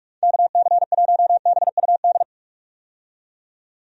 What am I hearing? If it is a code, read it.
Morse code, heard as a keyed tone, about 43 wpm, 700 Hz. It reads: KC1BUD
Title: USA Call Signs - 6 Characters - 45wpm